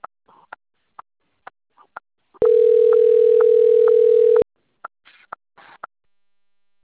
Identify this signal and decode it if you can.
Ringback tone (North America), 440 + 480 Hz.